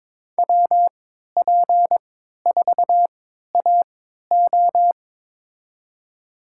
Morse code, heard as a keyed tone, about 22 wpm, spaced as for 13 wpm, 700 Hz. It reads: WP4AO
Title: USA Call Signs - 5 Characters - 3x Character Spacing - 22wpm